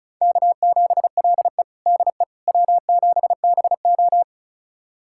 Morse code, heard as a keyed tone, about 36 wpm, 700 Hz. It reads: K7LE DE W7BO
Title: USA Callsign DE Callsign - 4 Characters - 35wpm